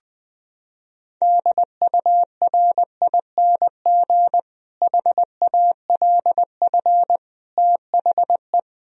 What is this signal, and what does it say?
Morse code, heard as a keyed tone, about 21 wpm, 700 Hz. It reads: DURING HALF THE
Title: Sentences from Top 1000 Words - Review Part 2 - 20wpm